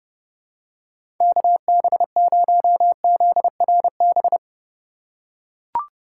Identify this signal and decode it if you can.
Morse code, heard as a keyed tone, about 32 wpm, 700 Hz. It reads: KB0ZRB
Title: USA Call Signs - 6 Characters - 4x Character Spacing - 30wpm